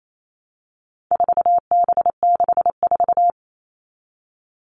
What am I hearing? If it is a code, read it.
Morse code, heard as a keyed tone, about 28 wpm, 700 Hz. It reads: 4B64
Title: All Cut Numbers - 4-Digit 28wpm